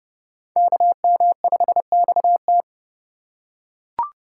Morse code, heard as a keyed tone, about 29 wpm, 700 Hz. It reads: KM5XT